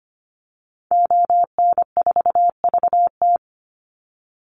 Morse code, heard as a keyed tone, about 25 wpm, 700 Hz. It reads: ON4VT
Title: Call Signs Repeated 3x - 25wpm